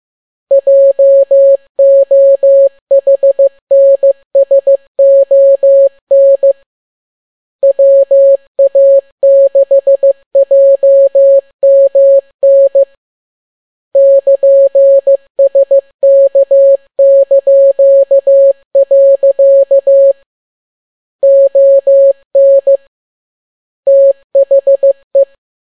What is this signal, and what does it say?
Morse code, heard as a keyed tone, about 15 wpm, 550 Hz. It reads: JOHNSON WA6JMN <KN>SK). ON THE
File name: n6tt p1 15C13